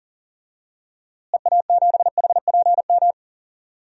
Morse code, heard as a keyed tone, about 40 wpm, 700 Hz. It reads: EA7HPM
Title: DXCC-100 Call Signs - 6 Characters - Repeated 3x - 40wpm